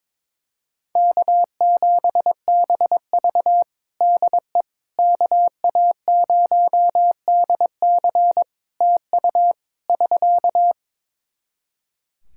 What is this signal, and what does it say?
Morse code, heard as a keyed tone, about 23 wpm, 700 Hz. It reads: K7BV DE KA0DC TU <SK>
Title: Practice QSO - Exchange 4 - Ending - 22wpm